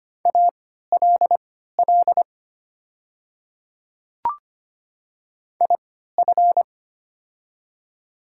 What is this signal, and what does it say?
Morse code, heard as a keyed tone, about 25 wpm, spaced as for 14 wpm, 700 Hz. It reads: ALL IF
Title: 3x Character Spacing - Top 200 Words - 25wpm